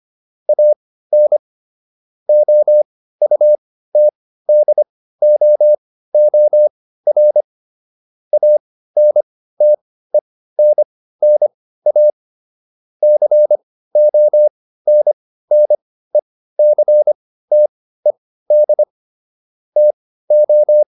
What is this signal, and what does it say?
Morse code, heard as a keyed tone, about 25 wpm, spaced as for 15 wpm, 600 Hz. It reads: AN OUTDOOR ANTENNA CONNECTED TO